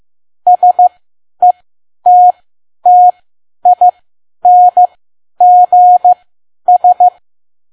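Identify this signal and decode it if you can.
Morse code, heard as a keyed tone, about 15 wpm, spaced as for 10 wpm, 720 Hz. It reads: SETTINGS
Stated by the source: Morse code practice at 10 WPM.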